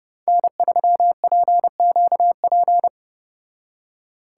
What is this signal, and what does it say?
Morse code, heard as a keyed tone, about 29 wpm, 700 Hz. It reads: N3PQP